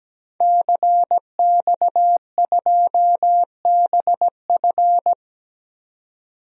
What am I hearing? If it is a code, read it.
Morse code, heard as a keyed tone, about 17 wpm, 700 Hz. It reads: CX2BF
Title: DXCC-100 Call Signs - 5 Characters - 17wpm